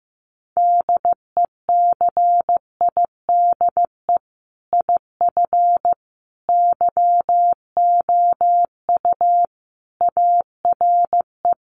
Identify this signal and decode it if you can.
Morse code, heard as a keyed tone, about 15 wpm, 700 Hz. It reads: DECIDE IF YOU ARE
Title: Sentences from Top 500 Words 15wpm